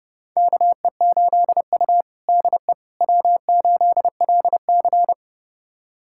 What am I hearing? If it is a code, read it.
Morse code, heard as a keyed tone, about 30 wpm, 700 Hz. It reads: KE8U DE W8LC
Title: USA Callsign DE Callsign - 4 Characters - 30wpm